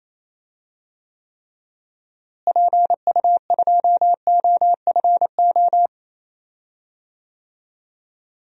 Morse code, heard as a keyed tone, about 28 wpm, 700 Hz. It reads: PU2OFO
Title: DXCC-100 Call Signs - 6 Characters - Repeated 3x - 28wpm